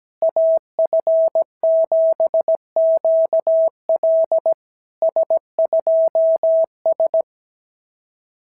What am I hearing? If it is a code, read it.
Morse code, heard as a keyed tone, about 17 wpm, 650 Hz. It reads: AF7QL S2S
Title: SOTA QSO Practice 17wpm